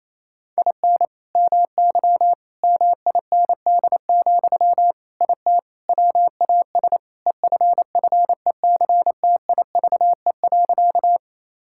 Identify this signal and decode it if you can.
Morse code, heard as a keyed tone, about 28 wpm, 700 Hz. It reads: IN MY MIND, IT WAS EFFECTIVE.
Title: Sentences from Top 1000 Words - Review Part 6 - 28wpm